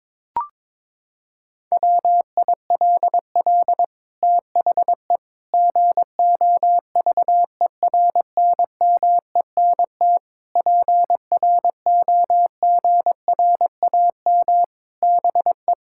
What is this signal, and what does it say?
Morse code, heard as a keyed tone, about 22 wpm, 700 Hz. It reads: WILL THE GOVERNMENT PROGRAM BE
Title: Sentences from Top 1000 Words - Review Part 5 - 22wpm